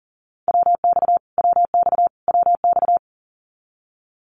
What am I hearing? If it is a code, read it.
Morse code, heard as a keyed tone, about 40 wpm, 700 Hz. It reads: WX WX WX
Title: Abbreviations Repeated 3x - 40wpm